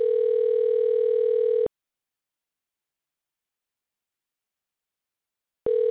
Ringback tone (North America), 440 + 480 Hz.